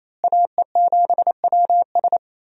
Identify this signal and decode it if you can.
Morse code, heard as a keyed tone, about 28 wpm, 700 Hz. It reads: AE7WS
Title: USA Call Signs - 5 Characters - Repeated 2x - 28wpm